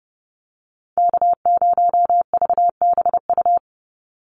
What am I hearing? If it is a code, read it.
Morse code, heard as a keyed tone, about 30 wpm, 700 Hz. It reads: K0VBU
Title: Call Signs - 30wpm